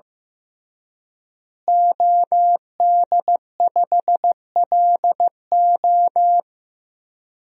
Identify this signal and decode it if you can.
Morse code, heard as a keyed tone, about 15 wpm, 700 Hz. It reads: OD5LO
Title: DXCC-100 Call Signs - 5 Characters - Repeated 3x - 15wpm